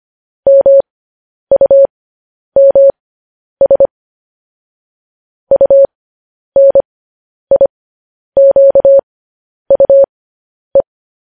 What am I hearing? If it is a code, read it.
Morse code, heard as a keyed tone, about 25 wpm, spaced as for 10 wpm, 550 Hz. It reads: MUMS UNIQUE
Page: news-headlines-in-morse-code-at-10-wpm